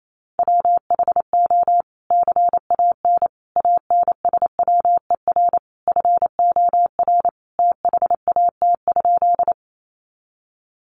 Morse code, heard as a keyed tone, about 28 wpm, 700 Hz. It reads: WHO CAN ANSWER FOR THAT?
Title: Sentences from Top 300 Words 28wpm